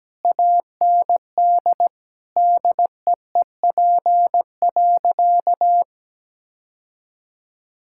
Morse code, heard as a keyed tone, about 17 wpm, 700 Hz. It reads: AND DEEP.
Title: Sentences from Top 1600 Words 17wpm